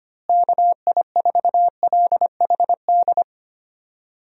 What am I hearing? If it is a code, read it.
Morse code, heard as a keyed tone, about 25 wpm, 700 Hz. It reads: KI4LHD
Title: USA Call Signs - 6 Characters - Repeated 2x - 25wpm